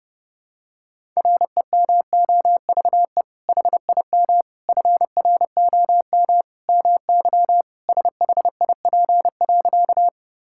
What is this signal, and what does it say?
Morse code, heard as a keyed tone, about 30 wpm, 700 Hz. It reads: REMOVE HIM FROM MY SHIP.
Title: Sentences from Top 1000 Words - Review Part 5 - 30wpm